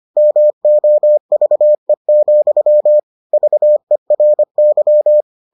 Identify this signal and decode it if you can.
Morse code, heard as a keyed tone, about 25 wpm, 600 Hz. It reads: MOVE, VERY